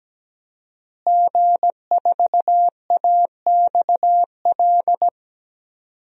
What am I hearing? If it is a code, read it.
Morse code, heard as a keyed tone, about 17 wpm, 700 Hz. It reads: G4AXL
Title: DXCC-100 Call Signs - 5 Characters - Repeated 3x - 17wpm